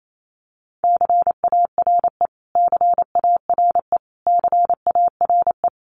Morse code, heard as a keyed tone, about 28 wpm, 700 Hz. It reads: CARE CARE CARE
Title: Top 500 Words Repeated 28wpm